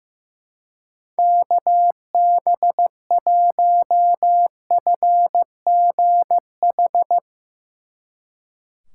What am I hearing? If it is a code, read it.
Morse code, heard as a keyed tone, about 17 wpm, 700 Hz. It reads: KB1FGH
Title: Speed Racing - USA 2x3 Call Signs 15wpm